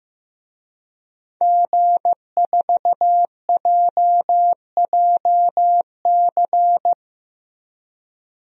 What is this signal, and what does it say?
Morse code, heard as a keyed tone, about 15 wpm, 700 Hz. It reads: G4JJC
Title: DXCC-100 Call Signs - 5 Characters - Repeated 2x - 15wpm